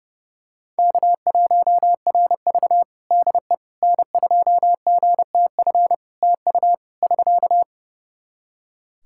Morse code, heard as a keyed tone, about 32 wpm, 700 Hz. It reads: K1RV DE N2GTF TU <SK>
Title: Practice QSO - Exchange 4 - Ending - 30wpm